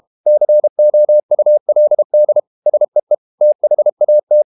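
Morse code, heard as a keyed tone, about 32 wpm, 600 Hz. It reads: COULD SEE THAT
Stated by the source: Morse code practice at 32 wpm from CW med Gnister, reading Ken Follett's novel Never in English.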